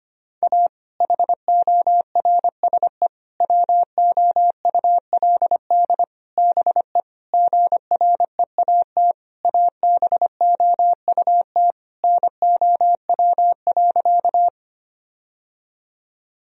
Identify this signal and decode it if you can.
Morse code, heard as a keyed tone, about 25 wpm, 700 Hz. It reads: A HORSE WOULD BE GREAT ABOUT NOW.
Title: Sentences from Top 1000 Words - Review Part 1 - 25wpm